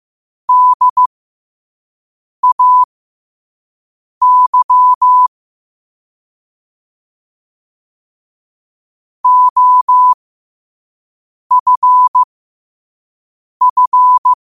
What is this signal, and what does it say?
Morse code, heard as a keyed tone, about 15 wpm, spaced as for 5 wpm, 1000 Hz. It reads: DAY OFF
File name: qotdfiles / 5wpm